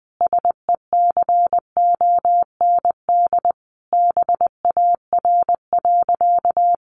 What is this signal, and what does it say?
Morse code, heard as a keyed tone, about 20 wpm, 700 Hz. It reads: SECOND BAR.
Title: Easy Phrases 20wpm